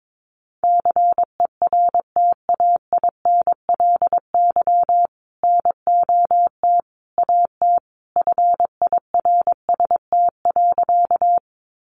Morse code, heard as a keyed tone, about 22 wpm, 700 Hz. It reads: CERTAINLY NOT AT FIRST.
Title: Sentences from Top 900 Words 22wpm